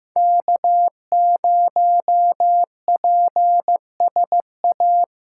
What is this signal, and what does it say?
Morse code, heard as a keyed tone, about 15 wpm, 700 Hz. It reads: K0PSA
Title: USA Callsign DE Callsign - 5 Characters - 15wpm